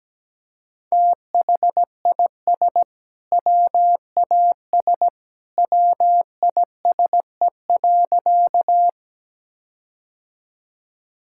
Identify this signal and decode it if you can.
Morse code, heard as a keyed tone, about 17 wpm, 700 Hz. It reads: THIS WAS WISE.
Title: Sentences from Top 1600 Words 17wpm